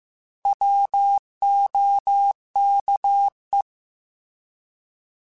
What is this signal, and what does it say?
Morse code, heard as a keyed tone, about 15 wpm, 800 Hz. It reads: WOKE
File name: Alphabetical / Alphabetical mp3s / Ws @13